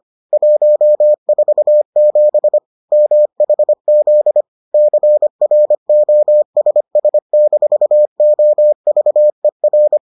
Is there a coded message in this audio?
Yes: Morse code from 0.3 to 10.0 s, heard as a keyed tone, about 25 wpm, 600 Hz. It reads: 147 MHZ CROSS-OVER